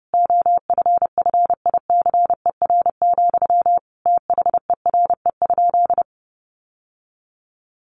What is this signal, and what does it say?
Morse code, heard as a keyed tone, about 30 wpm, 700 Hz. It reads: OFFICER, THERE?
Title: Sentences from Top 1200 Words 30wpm